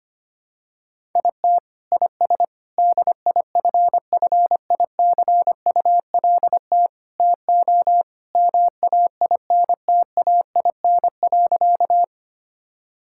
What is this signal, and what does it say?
Morse code, heard as a keyed tone, about 25 wpm, 700 Hz. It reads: IT IS DIFFICULT TO MAINTAIN.
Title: Sentences from Top 1000 Words - Review Part 5 - 25wpm